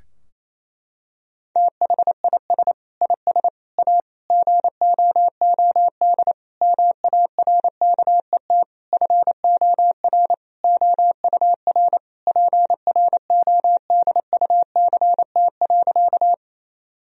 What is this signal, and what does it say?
Morse code, heard as a keyed tone, about 28 wpm, 700 Hz. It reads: THIS IS A GOOD MARKET FOR OUR PRODUCT.
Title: Sentences from Top 1000 Words - Review Part 3 - 28wpm